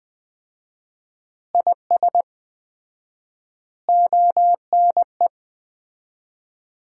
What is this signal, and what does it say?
Morse code, heard as a keyed tone, about 20 wpm, 700 Hz. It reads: IS ONE
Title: 4x Word-Spacing - Sets of 5 Words - 20wpm